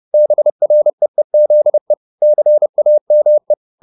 Morse code, heard as a keyed tone, about 30 wpm, 600 Hz. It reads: BREEZE CAME
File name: the old man and the sea / the old man and the sea 1 30wpm